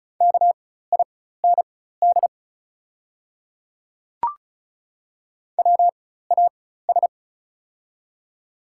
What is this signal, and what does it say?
Morse code, heard as a keyed tone, about 35 wpm, spaced as for 16 wpm, 690 Hz. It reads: KIND WAS
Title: 4x Character Spacing - Top 300 Words - 35wpm